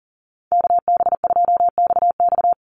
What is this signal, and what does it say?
Morse code, heard as a keyed tone, about 40 wpm, 700 Hz. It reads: KB2XX
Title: NA SOTA Chaser Callsigns 40wpm